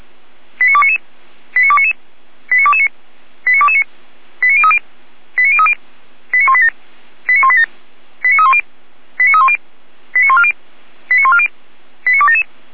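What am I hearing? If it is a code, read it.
Five-tone selective call (ZVEI), heard as quick runs of five tones, tones of about 70 ms. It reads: 89299 89299 89229 89229 89030 89030 89187 89187 89310 89310 89140 89140 89277
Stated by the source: ZVEI five-tone selective call received on SDR.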